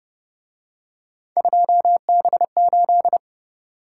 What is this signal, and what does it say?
Morse code, heard as a keyed tone, about 29 wpm, 700 Hz. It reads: 2B8